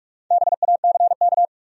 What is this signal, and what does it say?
Morse code, heard as a keyed tone, about 58 wpm, 700 Hz. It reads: BACK